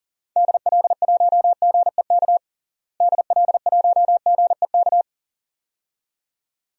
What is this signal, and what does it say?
Morse code, heard as a keyed tone, about 37 wpm, 700 Hz. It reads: DL1GEK DL1GEK
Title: DXCC-100 Call Signs - 6 Characters - Repeated 2x - 40wpm